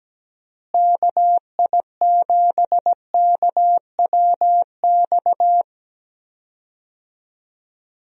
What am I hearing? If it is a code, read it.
Morse code, heard as a keyed tone, about 17 wpm, 700 Hz. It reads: KI7KWX
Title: USA Call Signs - 6 Characters - Repeated 3x - 17wpm